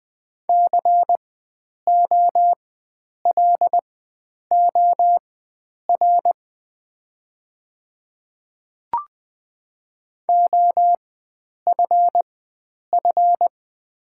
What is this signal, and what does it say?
Morse code, heard as a keyed tone, about 20 wpm, spaced as for 9 wpm, 700 Hz. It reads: COLOR OFF
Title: 4x Character Spacing - Top 300 Words - 20wpm